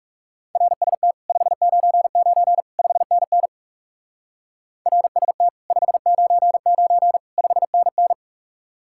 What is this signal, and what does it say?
Morse code, heard as a keyed tone, about 36 wpm, 700 Hz. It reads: RST 599 5NN RST 599 5NN
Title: Speed-Racing Common QSO Phrases Encore - 30wpm